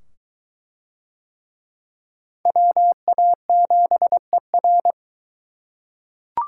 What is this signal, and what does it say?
Morse code, heard as a keyed tone, about 22 wpm, 700 Hz. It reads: WA7ER